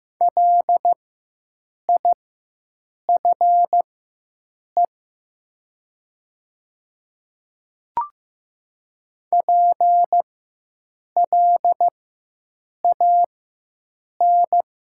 Morse code, heard as a keyed tone, about 15 wpm, spaced as for 7 wpm, 700 Hz. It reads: LIFE PLAN
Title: 4x Character Spacing - Top 500 Words - 15wpm